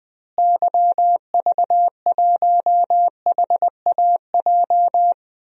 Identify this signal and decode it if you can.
Morse code, heard as a keyed tone, about 20 wpm, 700 Hz. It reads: YV1HAJ
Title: DXCC-100 Call Signs - 6 Characters - Repeated 2x - 20wpm